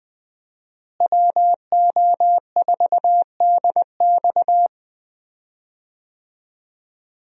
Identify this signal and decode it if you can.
Morse code, heard as a keyed tone, about 20 wpm, 700 Hz. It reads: WO4DX
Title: USA Call Signs - 5 Characters - Repeated 2x - 20wpm